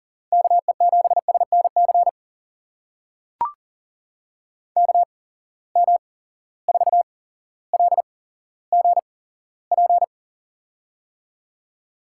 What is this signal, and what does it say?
Morse code, heard as a keyed tone, about 40 wpm, spaced as for 17 wpm, 700 Hz. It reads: KE7SNC KM4LGP
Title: USA Call Signs - 6 Characters - 8x Character Spacing - 40wpm